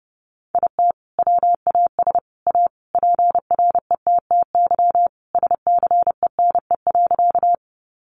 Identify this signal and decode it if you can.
Morse code, heard as a keyed tone, about 30 wpm, 700 Hz. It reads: IT WAS A PRETTY SCENE.
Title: Sentences from Top 1100 Words 30wpm